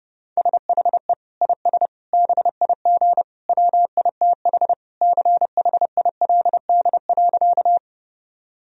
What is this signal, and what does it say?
Morse code, heard as a keyed tone, about 30 wpm, 700 Hz. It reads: SHE IS BIG WITH CHILD.
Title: Sentences from Top 1000 Words - Review Part 1 - 30wpm